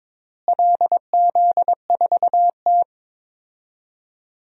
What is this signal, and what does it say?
Morse code, heard as a keyed tone, about 22 wpm, 700 Hz. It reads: LZ4T
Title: DXCC-100 Call Signs - 4 Characters - Repeated 3x - 22wpm